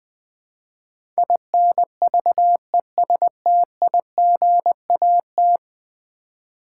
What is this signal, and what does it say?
Morse code, heard as a keyed tone, about 20 wpm, 700 Hz. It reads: INVESTIGAT
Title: Build A Word - 5 Letters At A Time - 12-Letter Words - 20wpm